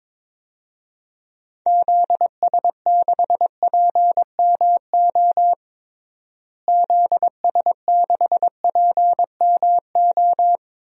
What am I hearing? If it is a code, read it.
Morse code, heard as a keyed tone, about 22 wpm, 700 Hz. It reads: ZS6PMO ZS6PMO
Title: DXCC-100 Call Signs - 6 Characters - Repeated 2x - 22wpm